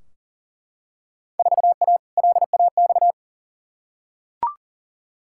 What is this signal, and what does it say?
Morse code, heard as a keyed tone, about 40 wpm, 700 Hz. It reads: 4A LAX
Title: Field Day Exchanges - 40wpm